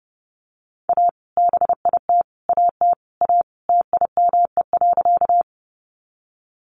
Morse code, heard as a keyed tone, about 30 wpm, 700 Hz. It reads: A BIT AT A TIME.
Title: Sentences from Top 900 Words 30wpm